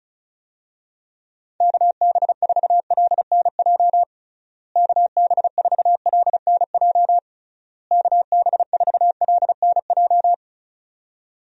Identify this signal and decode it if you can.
Morse code, heard as a keyed tone, about 36 wpm, 700 Hz. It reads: KB4LNJ KB4LNJ KB4LNJ
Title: USA Call Signs - 6 Characters - Repeated 3x - 35wpm